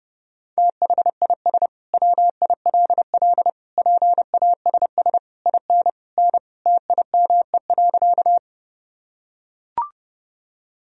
Morse code, heard as a keyed tone, about 32 wpm, 700 Hz. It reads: THIS WILL PASS IN N TIME.
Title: Sentences from Top 400 Words and Abbreviations 30wpm